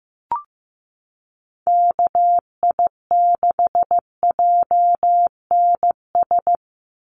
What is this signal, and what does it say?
Morse code, heard as a keyed tone, about 15 wpm, 700 Hz. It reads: KI6JNS
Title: USA Call Signs 15wpm